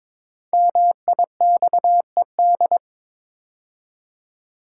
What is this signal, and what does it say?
Morse code, heard as a keyed tone, about 22 wpm, 700 Hz. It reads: MIXED
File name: 5 Letter Words - First 3 Letters Given - 22wpm